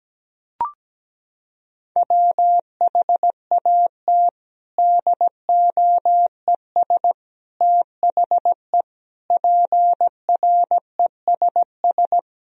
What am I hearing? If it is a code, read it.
Morse code, heard as a keyed tone, about 17 wpm, 700 Hz. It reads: WHAT DOES THE PRESS
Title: Sentences from Top 1000 Words - Review Part 3 - 17wpm